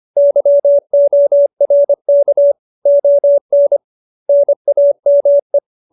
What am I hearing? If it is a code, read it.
Morse code, heard as a keyed tone, about 25 wpm, 570 Hz. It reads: YORK ON NAME